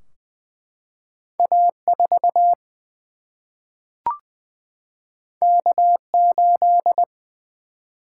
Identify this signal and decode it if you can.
Morse code, heard as a keyed tone, about 20 wpm, 700 Hz. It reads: A 4 K 8
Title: Single-Letter Single-Number 20wpm